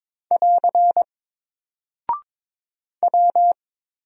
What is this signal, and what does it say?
Morse code, heard as a keyed tone, about 22 wpm, 700 Hz. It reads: <AR> W
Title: Single Character (Letter:Number:Punctuation:Prosign) 22wpm